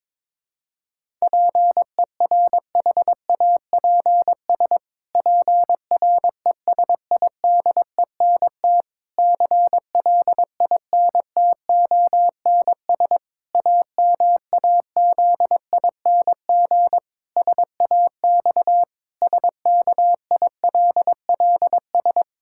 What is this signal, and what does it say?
Morse code, heard as a keyed tone, about 22 wpm, 700 Hz. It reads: PERHAPS PRESIDENT CLINTONS AMAZING SAX SKILLS
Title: Pangram Challenge v2 - 22wpm